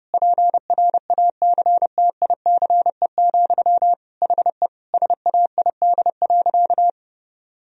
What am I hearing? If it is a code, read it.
Morse code, heard as a keyed tone, about 32 wpm, 700 Hz. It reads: PRACTICE, HE SAID.
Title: Sentences from Top 1000 Words - Review Part 4 - 30wpm